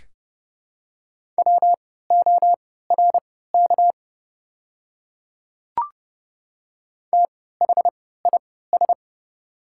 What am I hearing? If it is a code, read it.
Morse code, heard as a keyed tone, about 31 wpm, spaced as for 17 wpm, 700 Hz. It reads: WORK THIS
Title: 3x Character Spacing - Top 300 Words - 30wpm